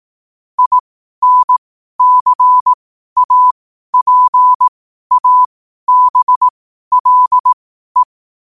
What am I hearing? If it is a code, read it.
Morse code, heard as a keyed tone, about 18 wpm, spaced as for 12 wpm, 1000 Hz. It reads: INCAPABLE
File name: qotdfiles / 12wpm